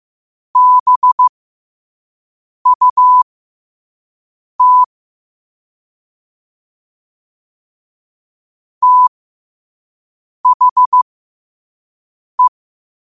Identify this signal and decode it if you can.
Morse code, heard as a keyed tone, about 15 wpm, spaced as for 5 wpm, 1000 Hz. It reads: BUT THE
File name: qotdfiles / 5wpm